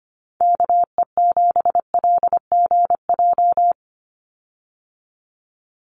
Morse code, heard as a keyed tone, about 25 wpm, 700 Hz. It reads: KE7LGJ
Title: USA Call Signs 25wpm